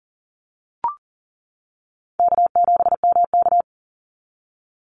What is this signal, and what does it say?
Morse code, heard as a keyed tone, about 40 wpm, 700 Hz. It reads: K7MK
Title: NA SOTA Activator Callsigns 40wpm